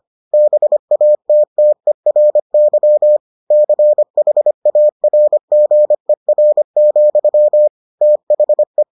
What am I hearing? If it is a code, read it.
Morse code, heard as a keyed tone, about 25 wpm, 600 Hz. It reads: BATTERY CHARGER, THE